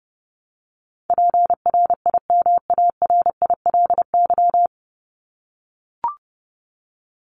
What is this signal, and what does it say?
Morse code, heard as a keyed tone, about 30 wpm, 700 Hz. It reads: PRIMARILY
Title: Top 1001-1250 Words 30wpm